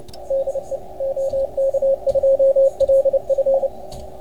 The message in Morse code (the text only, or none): DE OK2LF